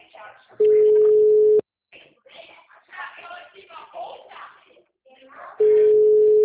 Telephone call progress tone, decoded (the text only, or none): ringback tone